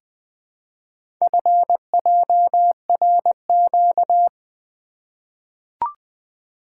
Morse code, heard as a keyed tone, about 20 wpm, 700 Hz. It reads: FJRQ
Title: Four Letter-Number Combinations - 20wpm